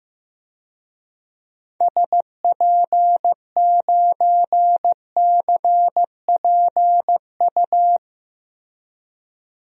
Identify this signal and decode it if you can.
Morse code, heard as a keyed tone, about 15 wpm, 700 Hz. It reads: SP9CPU
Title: DXCC-100 Call Signs - 6 Characters - Repeated 3x - 15wpm